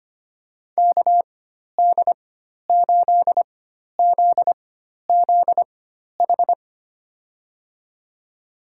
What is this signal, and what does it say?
Morse code, heard as a keyed tone, about 25 wpm, spaced as for 12 wpm, 700 Hz. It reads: KD8ZZH
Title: USA Call Signs - 6 Characters - 4x Character Spacing - 25wpm